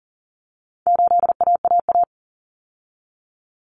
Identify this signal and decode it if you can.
Morse code, heard as a keyed tone, about 40 wpm, 700 Hz. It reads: 8AAA
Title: Common Cut Numbers - 4-Digit 40wpm